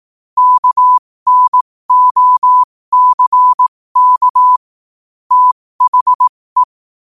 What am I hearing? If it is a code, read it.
Morse code, heard as a keyed tone, about 18 wpm, spaced as for 15 wpm, 1000 Hz. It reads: KNOCK THE
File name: qotdfiles / 15wpm